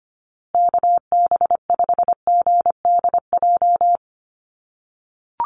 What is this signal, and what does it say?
Morse code, heard as a keyed tone, about 25 wpm, 700 Hz. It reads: KB5GDJ
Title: USA Call Signs Repeated 2x 25wpm